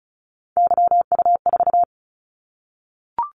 Morse code, heard as a keyed tone, about 36 wpm, 700 Hz. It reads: YU4
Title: Callsign Prefixes for Top 25 DXCC - 35wpm